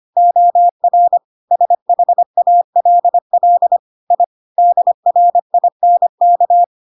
Morse code, heard as a keyed tone, about 25 wpm, 700 Hz. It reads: OR SHALL I DRINK
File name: Pinocchio / Pinocchio ch5 25 25 ik1hge